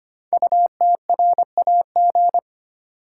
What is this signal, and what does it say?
Morse code, heard as a keyed tone, about 25 wpm, 700 Hz. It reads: UTRAG
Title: Code Groups - 5 Letter:Numbers - 25wpm